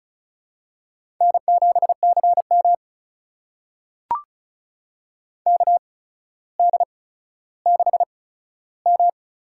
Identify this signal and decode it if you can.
Morse code, heard as a keyed tone, about 36 wpm, spaced as for 15 wpm, 700 Hz. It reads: N7CM KD6M
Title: USA Call Signs - 4 Characters - 8x Character Spacing - 35wpm